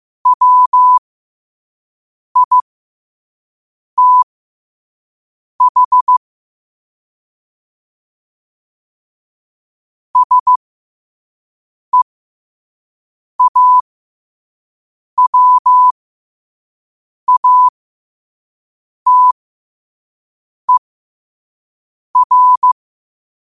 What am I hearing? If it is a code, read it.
Morse code, heard as a keyed tone, about 15 wpm, spaced as for 5 wpm, 1000 Hz. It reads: WITH SEAWATER
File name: 5wpm